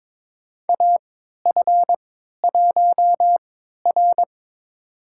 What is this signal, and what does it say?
Morse code, heard as a keyed tone, about 21 wpm, spaced as for 12 wpm, 700 Hz. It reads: AF1R
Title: USA Call Signs - 4 Characters - 3x Character Spacing - 22wpm